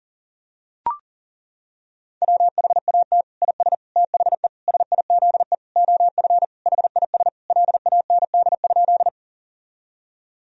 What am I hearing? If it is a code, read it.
Morse code, heard as a keyed tone, about 40 wpm, 700 Hz. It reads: WHAT IS THE SIZE OF HIS LAND?
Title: Sentences from Top 1000 Words - Review Part 2 - 40wpm